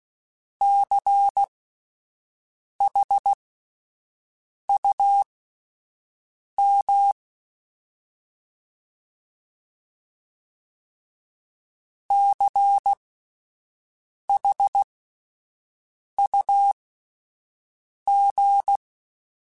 Morse code, heard as a keyed tone, about 16 wpm, spaced as for 5 wpm, 800 Hz. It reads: CHUM CHUG
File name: Spaced Alphas / Spaced Alphas mp3s / Cs sp @10